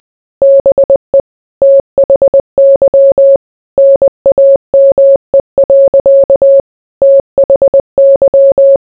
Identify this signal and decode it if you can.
Morse code, heard as a keyed tone, about 20 wpm, 550 Hz. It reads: BE THY NAME. THY